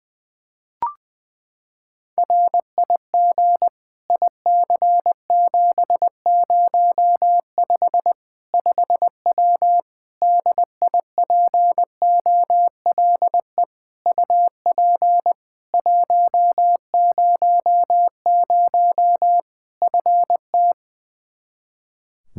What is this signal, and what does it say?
Morse code, heard as a keyed tone, about 21 wpm, 700 Hz. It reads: RIG IC705 5W DIPOLE UP 100 FT
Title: K1USN SST QSO Practice 20wpm